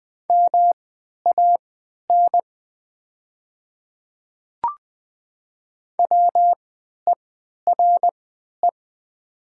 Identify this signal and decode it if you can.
Morse code, heard as a keyed tone, about 20 wpm, spaced as for 11 wpm, 700 Hz. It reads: MAN WERE
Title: 3x Character Spacing - Top 500 Words - 20wpm